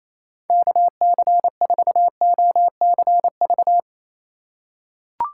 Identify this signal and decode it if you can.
Morse code, heard as a keyed tone, about 28 wpm, 700 Hz. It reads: KC4OCV
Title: USA Call Signs - 6 Characters - 3x Character Spacing - 28wpm